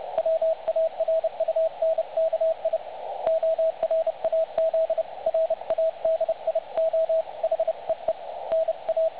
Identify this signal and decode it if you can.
Morse code, heard as a keyed tone, about 30 wpm, 660 Hz, with radio noise. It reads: WARUNKI ORAZ RADIO HEE NA
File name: dlazaawansowanych / nagraniazpasma / websdr recording 3565.0kHz